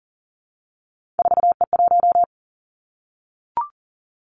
Morse code, heard as a keyed tone, about 40 wpm, 700 Hz. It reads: 4E1
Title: Three Challenging Letter-Number Combinations 40wpm